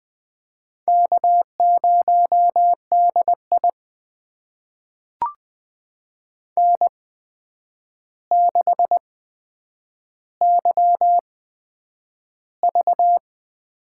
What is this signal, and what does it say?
Morse code, heard as a keyed tone, about 20 wpm, spaced as for 9 wpm, 700 Hz. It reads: K0DI N6YV
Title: USA Call Signs - 4 Characters - 8x Character Spacing - 20wpm